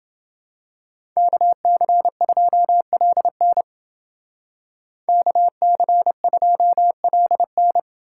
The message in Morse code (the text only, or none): KC2LN KC2LN